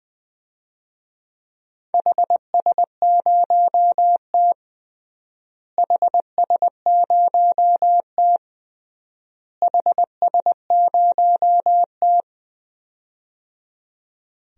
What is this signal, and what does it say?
Morse code, heard as a keyed tone, about 21 wpm, 700 Hz. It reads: HS0T HS0T HS0T
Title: DXCC-100 Call Signs - 4 Characters - Repeated 3x - 20wpm